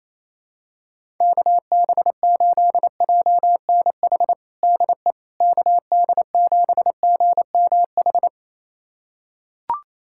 Morse code, heard as a keyed tone, about 28 wpm, 700 Hz. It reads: KB8JNH DE KD7GMH
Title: USA Callsign DE Callsign - 6 Characters - 28wpm